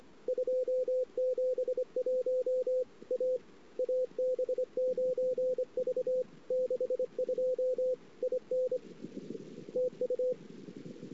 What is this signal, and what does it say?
Morse code, heard as a keyed tone, about 23 wpm, 510 Hz. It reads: 271 A AB9V 62 IN TU
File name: CW / 2008-ss-cw-k5tr-op-n5rz -19:00